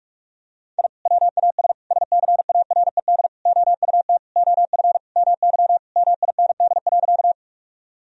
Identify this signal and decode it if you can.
Morse code, heard as a keyed tone, about 41 wpm, 700 Hz. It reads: I WAS SCARED OUT OF MY MIND.
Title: Sentences from Top 1000 Words - Review Part 4 - 45wpm